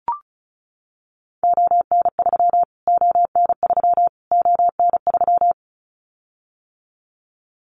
Morse code, heard as a keyed tone, about 34 wpm, 700 Hz. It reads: ON3 ON3 ON3
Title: Callsign Prefixes for Top 25 DXCC - Repeated 3x - 35wpm